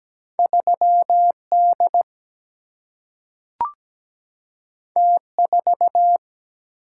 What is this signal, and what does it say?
Morse code, heard as a keyed tone, about 17 wpm, 700 Hz. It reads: 3D T4
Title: Single-Letter Single-Number 17wpm